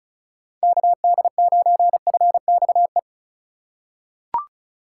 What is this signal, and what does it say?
Morse code, heard as a keyed tone, about 36 wpm, 700 Hz. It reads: KD9FXE
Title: USA Call Signs - 2x Character Spacing - 35wpm